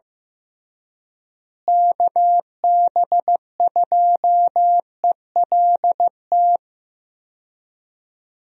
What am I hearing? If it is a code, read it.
Morse code, heard as a keyed tone, about 15 wpm, 700 Hz. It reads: KB2ELT
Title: USA Call Signs - 6 Characters - Repeated 2x - 15wpm